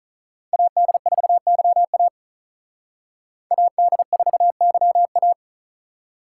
Morse code, heard as a keyed tone, about 46 wpm, 700 Hz. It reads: AD4YA AD4YA